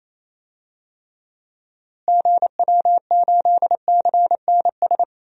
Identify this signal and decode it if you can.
Morse code, heard as a keyed tone, about 28 wpm, 700 Hz. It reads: GW8CNS
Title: DXCC-100 Call Signs - 6 Characters - Repeated 2x - 28wpm